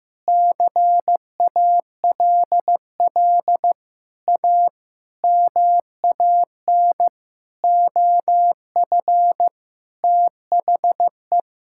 Morse code, heard as a keyed tone, about 15 wpm, 700 Hz. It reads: CALL A MAN OF THE
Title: Sentences from Top 1600 Words 15wpm